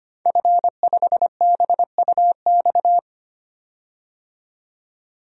Morse code, heard as a keyed tone, about 25 wpm, 700 Hz. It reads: F5BUX
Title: DXCC-100 Call Signs - 5 Characters - 25wpm